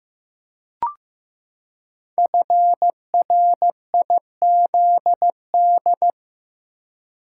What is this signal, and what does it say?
Morse code, heard as a keyed tone, about 15 wpm, 700 Hz. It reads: FRIZD
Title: Code Groups - 5 Letter:Numbers - 15wpm